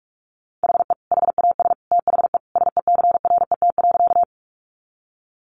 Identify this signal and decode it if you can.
Morse code, heard as a keyed tone, about 44 wpm, 700 Hz. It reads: HE HAS THE SECRET.
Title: Sentences from Top 1200 Words 45wpm